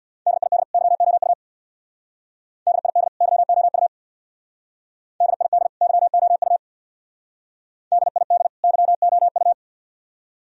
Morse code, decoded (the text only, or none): DID YOU DID YOU DID YOU DID YOU